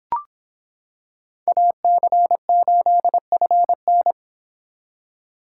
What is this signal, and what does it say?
Morse code, heard as a keyed tone, about 26 wpm, 700 Hz. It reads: AC8FN